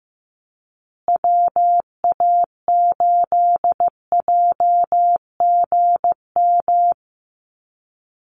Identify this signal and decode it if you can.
Morse code, heard as a keyed tone, about 15 wpm, 700 Hz. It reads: WA8JGM
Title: USA Call Signs Repeated 2x 15wpm